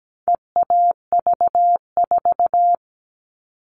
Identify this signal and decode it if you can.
Morse code, heard as a keyed tone, about 17 wpm, 700 Hz. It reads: EAV4
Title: All Cut Numbers - 4-Digit 17wpm